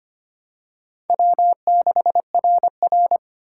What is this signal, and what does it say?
Morse code, heard as a keyed tone, about 25 wpm, 700 Hz. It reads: W6RR
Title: USA Call Signs - 4 Characters - 4x Character Spacing - 25wpm